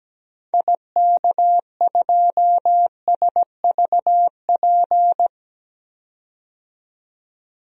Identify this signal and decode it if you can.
Morse code, heard as a keyed tone, about 17 wpm, 700 Hz. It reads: IK2SVP
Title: DXCC-100 Call Signs - 6 Characters - 17wpm